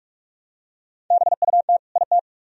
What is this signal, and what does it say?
Morse code, heard as a keyed tone, about 44 wpm, 700 Hz. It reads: BUT IT